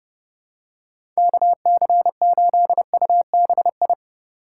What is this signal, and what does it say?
Morse code, heard as a keyed tone, about 32 wpm, 700 Hz. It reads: KC8UBI
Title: USA Call Signs - 4x Character Spacing - 30wpm